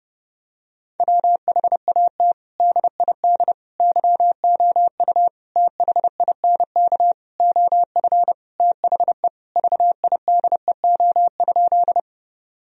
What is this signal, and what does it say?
Morse code, heard as a keyed tone, about 32 wpm, 700 Hz. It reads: WHAT DID YOU THINK OF THE VIDEO?
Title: Sentences from Top 1000 Words - Review Part 4 - 30wpm